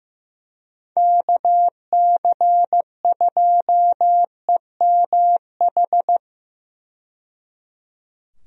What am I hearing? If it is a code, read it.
Morse code, heard as a keyed tone, about 16 wpm, 700 Hz. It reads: KC2EMH
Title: USA Call Signs - 6 Characters - 15wpm